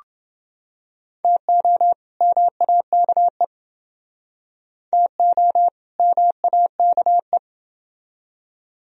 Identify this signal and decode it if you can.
Morse code, heard as a keyed tone, about 29 wpm, 700 Hz. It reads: TO MAKE TO MAKE